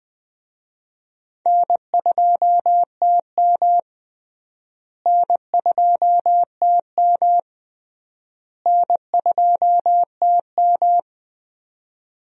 Morse code, heard as a keyed tone, about 20 wpm, 700 Hz. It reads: N2TM N2TM N2TM
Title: USA Call Signs - 4 Characters - Repeated 3x - 20wpm